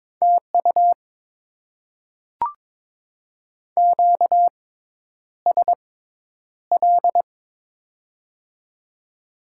Morse code, heard as a keyed tone, about 22 wpm, spaced as for 10 wpm, 700 Hz. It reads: TU QSL
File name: Top 90 QSO Elements - 6x Character Spacing - 22wpm